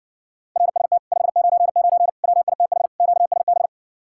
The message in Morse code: RST 599 WITH QSB

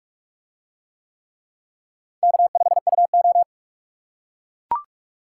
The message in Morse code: K5UO

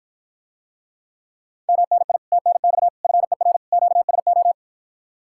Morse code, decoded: MNI TNX FER QSO

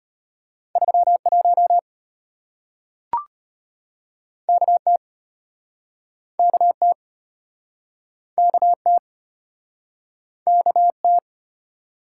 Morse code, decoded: 31 KT KT KT KT

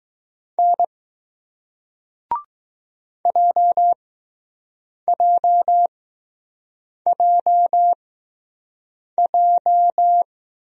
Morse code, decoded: N JJJJ

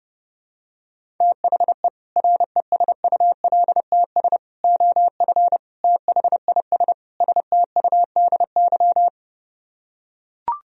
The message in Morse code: THE RESULTS OF THIS STUDY

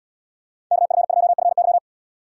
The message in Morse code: KF2FY

